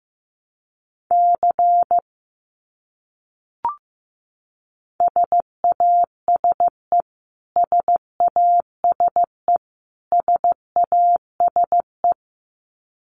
Morse code, heard as a keyed tone, about 15 wpm, 700 Hz. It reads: C SASE SASE SASE
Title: Abbreviations Repeated 3x - 15wpm